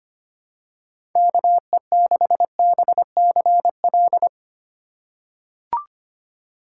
Morse code, heard as a keyed tone, about 25 wpm, 700 Hz. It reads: KE6BCL
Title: USA Call Signs - 6 Characters - 2x Character Spacing - 25wpm